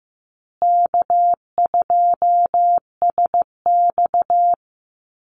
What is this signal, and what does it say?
Morse code, heard as a keyed tone, about 15 wpm, 700 Hz. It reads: K2SX
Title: Call Signs Repeated 3x - 15wpm